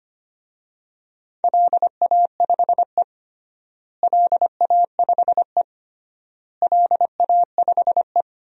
Morse code, heard as a keyed tone, about 25 wpm, 700 Hz. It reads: LA5E LA5E LA5E
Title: DXCC-100 Call Signs - 4 Characters - Repeated 3x - 25wpm